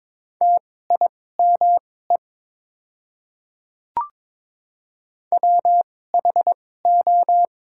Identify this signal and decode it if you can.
Morse code, heard as a keyed tone, about 22 wpm, spaced as for 16 wpm, 700 Hz. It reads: TIME WHO
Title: 2x Character Spacing - Top 100 Words - 22wpm